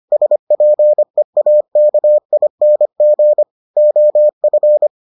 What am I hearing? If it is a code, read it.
Morse code, heard as a keyed tone, about 25 wpm, 600 Hz. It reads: SPEAKING OF